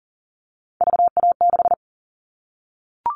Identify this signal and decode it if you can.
Morse code, heard as a keyed tone, about 40 wpm, 700 Hz. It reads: VA6
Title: All Cut Numbers - 3-Digit 40wpm